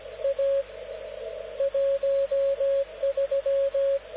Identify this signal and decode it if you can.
Morse code, heard as a keyed tone, about 17 wpm, 550 Hz, with radio noise. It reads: A 13
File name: luxorion / Radio / dk0wcy-spaceweather-10144khz